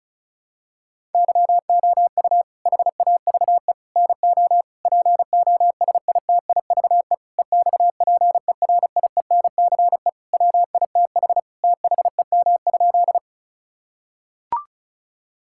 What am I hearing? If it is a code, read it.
Morse code, heard as a keyed tone, about 34 wpm, 700 Hz. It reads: YOU HAVE NO POSITIVE EXPERIENCE WITH THEM?
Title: Sentences from Top 1000 Words - Review Part 5 - 35wpm